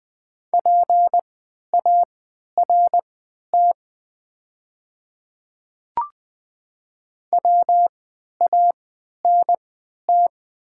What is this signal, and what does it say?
Morse code, heard as a keyed tone, about 20 wpm, spaced as for 11 wpm, 700 Hz. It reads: PART WANT
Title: 3x Character Spacing - Top 300 Words - 20wpm